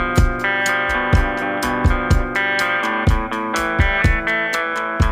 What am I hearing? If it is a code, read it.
Emergency Alert System (SAME) tones (header bursts), heard as harsh buzzing bursts. It is heard from 0.4 to 4.9 s.